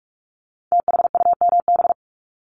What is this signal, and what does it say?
Morse code, heard as a keyed tone, about 46 wpm, 700 Hz. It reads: THUMB
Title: Multi-letter Phonemes Repeated 3x 45wpm